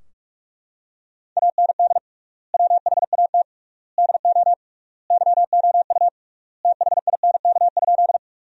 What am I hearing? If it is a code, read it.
Morse code, heard as a keyed tone, about 44 wpm, 700 Hz. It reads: AND WHAT DO YOU THINK?
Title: 3x Word-Spacing - Sentences from Top 200 Words - 45wpm